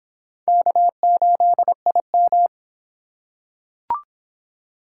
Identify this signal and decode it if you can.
Morse code, heard as a keyed tone, about 25 wpm, 700 Hz. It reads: K8IM